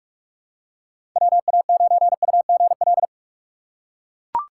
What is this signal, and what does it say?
Morse code, heard as a keyed tone, about 44 wpm, 700 Hz. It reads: WA9UGL
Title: USA Call Signs - 4x Character Spacing - 45wpm